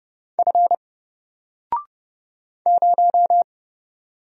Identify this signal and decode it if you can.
Morse code, heard as a keyed tone, about 30 wpm, 700 Hz. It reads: F 0
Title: Single Character (Letter:Number:Punctuation:Prosign) Alt 30wpm